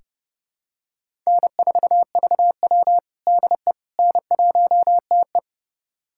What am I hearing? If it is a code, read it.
Morse code, heard as a keyed tone, about 30 wpm, 700 Hz. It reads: N4VW DE N1TE
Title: USA Callsign DE Callsign - 4 Characters - 30wpm